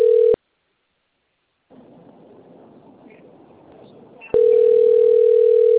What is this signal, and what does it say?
Ringback tone (North America), 440 + 480 Hz.